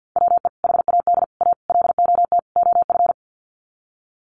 Morse code, heard as a keyed tone, about 50 wpm, 700 Hz. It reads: WE HAD A LOT OF
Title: Sets of 5 Words Encore 50wpm